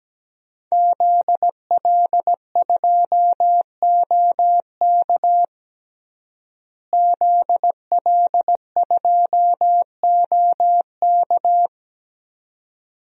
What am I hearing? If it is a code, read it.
Morse code, heard as a keyed tone, about 17 wpm, 700 Hz. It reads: ZL2OK ZL2OK
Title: DXCC-100 Call Signs - 5 Characters - Repeated 3x - 17wpm